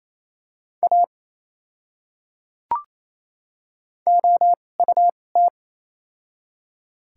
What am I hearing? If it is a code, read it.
Morse code, heard as a keyed tone, about 28 wpm, spaced as for 7 wpm, 700 Hz. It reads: AOUT
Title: 2x Character Spacing - Top 200 Words - 28wpm